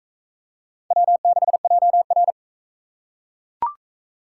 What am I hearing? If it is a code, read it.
Morse code, heard as a keyed tone, about 41 wpm, 700 Hz. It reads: W6JR